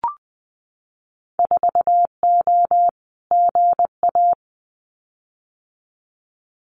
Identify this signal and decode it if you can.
Morse code, heard as a keyed tone, about 20 wpm, 700 Hz. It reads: 4O GA
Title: Winter Field Day Exchanges 20wpm